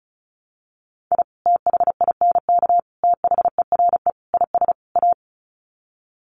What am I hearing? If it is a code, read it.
Morse code, heard as a keyed tone, about 34 wpm, 700 Hz. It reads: I THINK THERE IS A
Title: Sets of 5 Words 35wpm